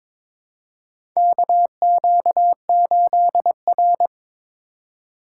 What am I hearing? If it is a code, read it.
Morse code, heard as a keyed tone, about 22 wpm, 700 Hz. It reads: KQ8R
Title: USA Call Signs - 2x Character Spacing - 22wpm